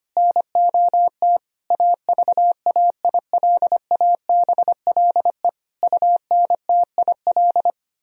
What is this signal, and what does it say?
Morse code, heard as a keyed tone, about 25 wpm, 700 Hz. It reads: NOT AVAILABLE UNTIL
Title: Sentences from Top 1000 Words - Review Part 3 - 25wpm